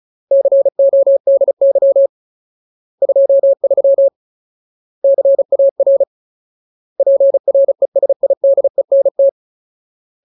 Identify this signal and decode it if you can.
Morse code, heard as a keyed tone, about 34 wpm, 550 Hz. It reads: CODY 23 CAR PRESIDENT
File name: Ultimate Word List / 35WPM Ultimate